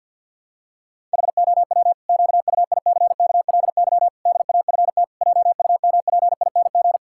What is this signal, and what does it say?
Morse code, heard as a keyed tone, about 50 wpm, 700 Hz. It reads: HOW QUICKLY DAFT JUMPING
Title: Pangram Challenge v2 - 50wpm